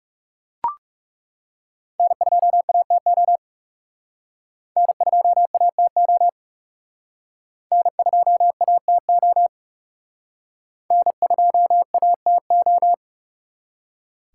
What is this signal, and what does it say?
Morse code, heard as a keyed tone, about 37 wpm, 700 Hz. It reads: N2ATO N2ATO N2ATO N2ATO
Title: Speed Racing - USA 1x3 Call Signs 30wpm